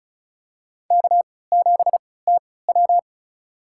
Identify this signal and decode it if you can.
Morse code, heard as a keyed tone, about 34 wpm, spaced as for 20 wpm, 700 Hz. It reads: K7TW
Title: USA Call Signs - 4 Characters - 3x Character Spacing - 35wpm